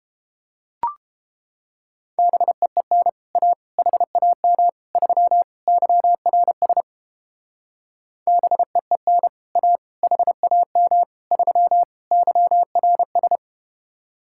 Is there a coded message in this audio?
Yes: Morse code from 2.2 to 13.4 s, heard as a keyed tone, about 32 wpm, 700 Hz. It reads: BEEN A HAM 3 YRS BEEN A HAM 3 YRS